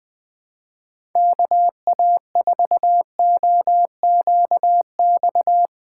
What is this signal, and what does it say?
Morse code, heard as a keyed tone, about 20 wpm, 700 Hz. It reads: KA4OQX
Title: USA Call Signs - 6 Characters - 4x Character Spacing - 20wpm